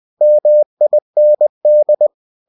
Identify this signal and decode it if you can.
Morse code, heard as a keyed tone, about 20 wpm, 600 Hz. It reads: MIND
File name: cw-books / winnie-the-pooh / winnie-20wpm-page-0037